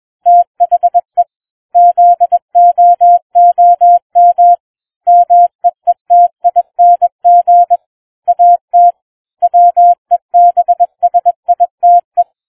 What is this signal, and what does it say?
Morse code, heard as a keyed tone, about 21 wpm, 700 Hz. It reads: THE ZOOM MEETING AT WEBSITE